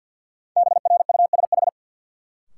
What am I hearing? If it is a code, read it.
Morse code, heard as a keyed tone, about 52 wpm, 700 Hz. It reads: BRUSH
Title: Sentences from Top 1100 Words - New Word Drill Alt - 50wpm